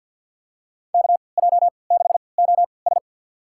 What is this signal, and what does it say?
Morse code, heard as a keyed tone, about 47 wpm, spaced as for 28 wpm, 700 Hz. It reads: KJ6OS
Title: USA Call Signs - 5 Characters - 3x Character Spacing - 50wpm